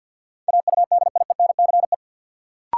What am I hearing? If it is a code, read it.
Morse code, heard as a keyed tone, about 53 wpm, 700 Hz. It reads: AUDIENCE
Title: Sentences from Top 800 Words - New Word Drill - 50wpm